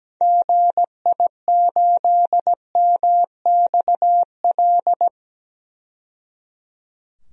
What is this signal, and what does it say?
Morse code, heard as a keyed tone, about 17 wpm, 700 Hz. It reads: GI8MXL
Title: DXCC-100 Call Signs - 6 Characters - 17wpm